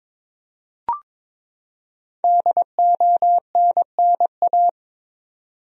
Morse code, heard as a keyed tone, about 23 wpm, 700 Hz. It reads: DONNA
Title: USA Common Names - Max 5 Letters - 22wpm